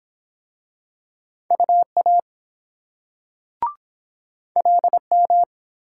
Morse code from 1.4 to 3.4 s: UA

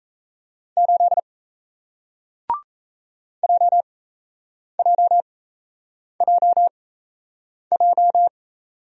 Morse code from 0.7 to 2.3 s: 8

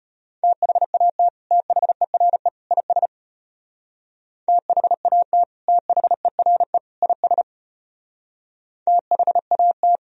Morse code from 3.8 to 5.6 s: THAT

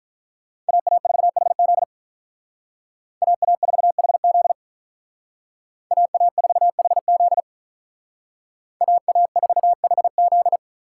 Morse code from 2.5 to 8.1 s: AA4HZ AA4HZ

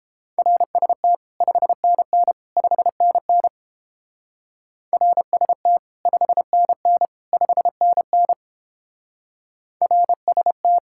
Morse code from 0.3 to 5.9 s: RST 5NN 5NN RST

